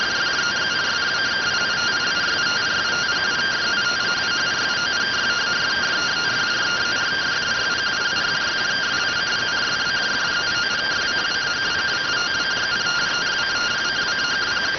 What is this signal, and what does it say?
Two-tone teleprinter signal, about 170 Hz shift.